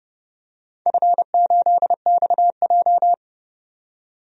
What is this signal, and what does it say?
Morse code, heard as a keyed tone, about 31 wpm, 700 Hz. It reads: F8XJ
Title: DXCC-100 Call Signs - 4 Characters - 30wpm